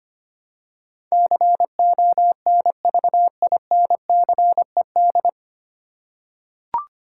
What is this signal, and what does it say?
Morse code, heard as a keyed tone, about 25 wpm, 700 Hz. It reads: CONVINCED
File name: Nine Letter Words - 25wpm